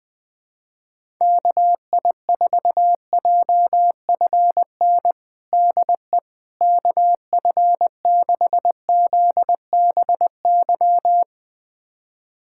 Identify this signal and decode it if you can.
Morse code, heard as a keyed tone, about 20 wpm, 700 Hz. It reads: KI4JFN DE KF6ZBY
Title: USA Callsign DE Callsign - 6 Characters - 20wpm